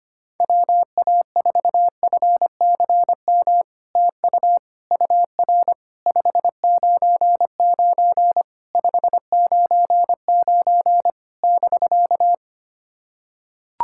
Morse code, heard as a keyed tone, about 25 wpm, 700 Hz. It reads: WA4FCM TU UR 599 599 <BK>
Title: SOTA QSO Practice 25wpm